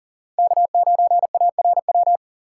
Morse code, heard as a keyed tone, about 39 wpm, 700 Hz. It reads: K9ARW
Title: USA Call Signs - 4x Character Spacing - 40wpm